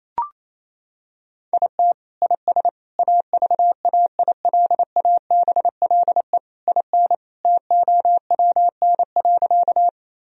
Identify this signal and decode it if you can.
Morse code, heard as a keyed tone, about 28 wpm, 700 Hz. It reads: IT IS AVAILABLE IN TOWN.
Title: Sentences from Top 1000 Words - Review Part 3 - 28wpm